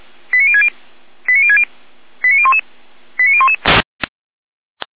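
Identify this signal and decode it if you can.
Five-tone selective call (ZVEI), heard as quick runs of five tones, tones of about 70 ms. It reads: 89060 89060 89011 89011
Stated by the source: ZVEI five-tone selective call received on an SDR, from feuerwehr-orlamuende.org fire brigade audio archive.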